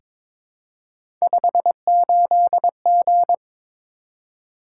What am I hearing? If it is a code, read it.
Morse code, heard as a keyed tone, about 22 wpm, 700 Hz. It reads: 58G
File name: practice / 22wpm / Three Letter-Number Combinations - 22wpm